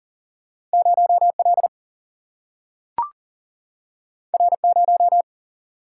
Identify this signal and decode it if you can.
Morse code, heard as a keyed tone, about 40 wpm, 700 Hz. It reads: 0 L R 0
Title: Single-Letter Single-Number 40wpm